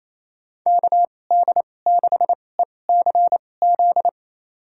Morse code, heard as a keyed tone, about 28 wpm, spaced as for 20 wpm, 700 Hz. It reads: KD6ECZ
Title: USA Call Signs - 6 Characters - 2x Character Spacing - 28wpm